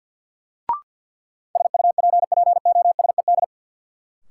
Morse code, heard as a keyed tone, about 49 wpm, 700 Hz. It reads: SUPPOSED
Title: Sets of 3 Words - New Word Drill - 50wpm